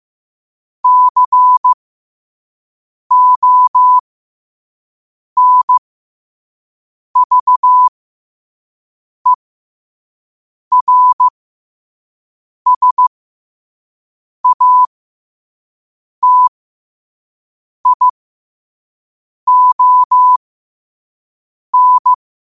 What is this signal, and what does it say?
Morse code, heard as a keyed tone, about 15 wpm, spaced as for 5 wpm, 1000 Hz. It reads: CONVERSATION
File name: qotdfiles / 5wpm